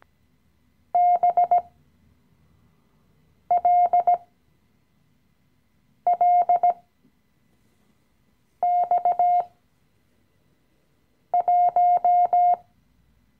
Morse code, heard as a keyed tone, about 17 wpm, spaced as for 4 wpm, 710 Hz. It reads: BLLX1
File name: CW-Online-Kurs / Anfänger / Lektion 015